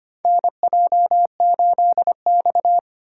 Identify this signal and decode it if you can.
Morse code, heard as a keyed tone, about 25 wpm, 700 Hz. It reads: NJ8X
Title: USA Call Signs - 4 Characters - 2x Character Spacing - 25wpm